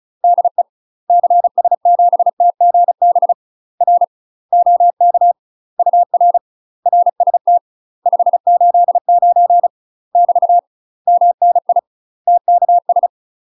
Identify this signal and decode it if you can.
Morse code, heard as a keyed tone, about 34 wpm, 700 Hz. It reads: DE CS7TGB R OK UR RST 589 <BT> MNI TKS
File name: UK MASTER QSOs TRAINING / 35WPM / UK MASTER QSOs 0006